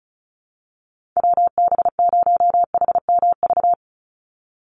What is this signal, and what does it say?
Morse code, heard as a keyed tone, about 34 wpm, 700 Hz. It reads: WB0HMV
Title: USA Call Signs 35wpm